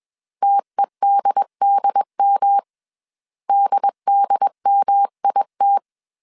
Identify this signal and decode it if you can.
Morse code, heard as a keyed tone, about 21 wpm, 800 Hz. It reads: TEDDM DDMIT